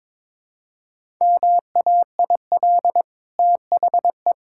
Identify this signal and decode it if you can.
Morse code, heard as a keyed tone, about 22 wpm, 690 Hz. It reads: MAIL THE
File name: practice / 22wpm / Easy Sentences with Hints - 22wpm